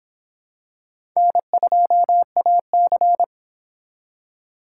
Morse code, heard as a keyed tone, about 25 wpm, 700 Hz. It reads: N2AC